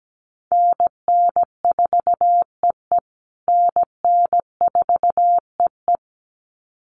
Morse code, heard as a keyed tone, about 17 wpm, 700 Hz. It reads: NN4EE NN4EE
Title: USA Call Signs Repeated 2x 17wpm